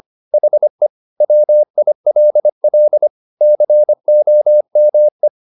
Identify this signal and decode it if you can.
Morse code, heard as a keyed tone, about 25 wpm, 600 Hz. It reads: HE WILL COME
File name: the old man and the sea part 1 25wpm